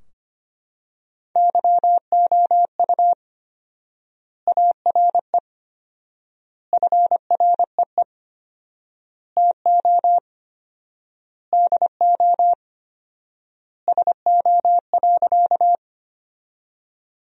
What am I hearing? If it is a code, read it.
Morse code, heard as a keyed tone, about 25 wpm, 700 Hz. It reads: YOU ARE FREE TO DO SO.
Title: 4x Word-Spacing - Sentences from Top 500 Words - 25wpm